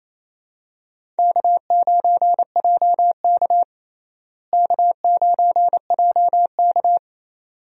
Morse code, heard as a keyed tone, about 28 wpm, 700 Hz. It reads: K9JK K9JK
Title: USA Call Signs - 4 Characters - Repeated 2x - 28wpm